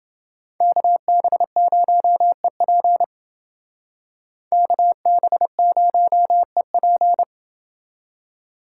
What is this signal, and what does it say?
Morse code, heard as a keyed tone, about 29 wpm, 700 Hz. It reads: KB0EP KB0EP